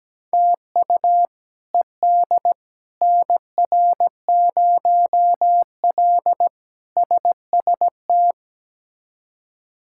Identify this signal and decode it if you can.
Morse code, heard as a keyed tone, about 17 wpm, 700 Hz. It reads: TU ED NR0L SST
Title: K1USN SST QSO Practice - 17wpm